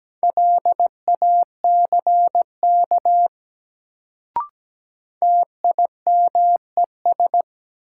Morse code, heard as a keyed tone, about 17 wpm, 700 Hz. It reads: LACK TIMES
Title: Sets of 3 Words - New Word Drill - 17wpm